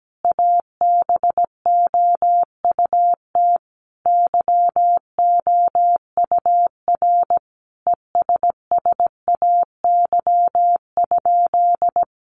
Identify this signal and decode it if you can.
Morse code, heard as a keyed tone, about 17 wpm, 700 Hz. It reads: ABOUT YOUR ESSAY?
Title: Sentences from Top 1500 Words 17wpm